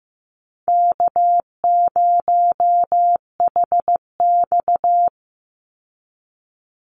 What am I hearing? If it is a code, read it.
Morse code, heard as a keyed tone, about 15 wpm, 700 Hz. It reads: K0HX
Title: Easy Call Signs Repeated 2x - 15wpm